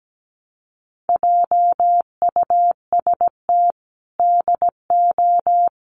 Morse code, heard as a keyed tone, about 17 wpm, 700 Hz. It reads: JUST DO
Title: Sets of 5 Words Encore 17wpm